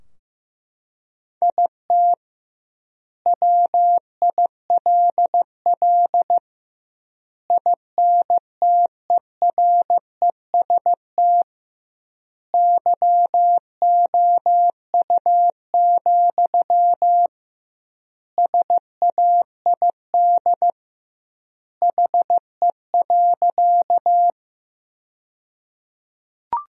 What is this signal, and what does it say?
Morse code, heard as a keyed tone, about 16 wpm, 700 Hz. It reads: IT WILL INTEREST YOU, SAID HE.
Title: 2x Word-Spacing - Sentences from Top 400 Words - 15wpm